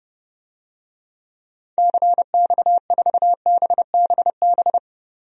Morse code, heard as a keyed tone, about 32 wpm, 700 Hz. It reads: CX4BBB
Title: DXCC-100 Call Signs - 6 Characters - 30wpm